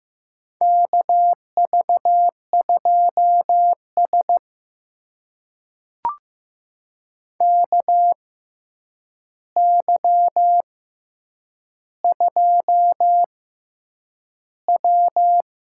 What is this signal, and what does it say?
Morse code, heard as a keyed tone, about 15 wpm, spaced as for 8 wpm, 700 Hz. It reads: KV2S KY2W
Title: USA Call Signs - 4 Characters - 6x Character Spacing - 15wpm